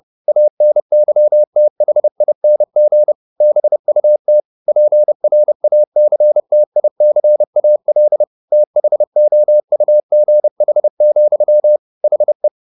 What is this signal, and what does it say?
Morse code, heard as a keyed tone, about 30 wpm, 600 Hz. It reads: ANYTHING BUT PRACTICAL THOUGH, HE
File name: the old man and the sea / the old man and the sea 1 30wpm